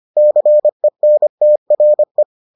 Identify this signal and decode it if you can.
Morse code, heard as a keyed tone, about 25 wpm, 600 Hz. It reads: CENTRE